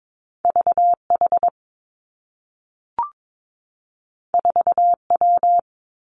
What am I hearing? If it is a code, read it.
Morse code, heard as a keyed tone, about 22 wpm, 700 Hz. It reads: V H 4 W
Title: Two Challenging Letter-Number Combinations 22wpm